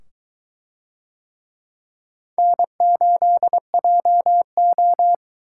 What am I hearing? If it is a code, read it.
Morse code, heard as a keyed tone, about 22 wpm, 700 Hz. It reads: N8JO